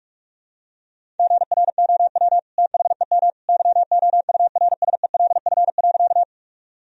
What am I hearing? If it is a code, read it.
Morse code, heard as a keyed tone, about 45 wpm, 700 Hz. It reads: GROW THEM YOURSELF.
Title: Sentences from Top 1000 Words - Review Part 1 - 45wpm